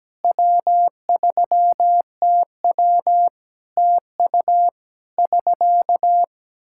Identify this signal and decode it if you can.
Morse code, heard as a keyed tone, about 17 wpm, 700 Hz. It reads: W3TW TU <SK>
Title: Practice QSO - Exchange 4 - Ending - 17wpm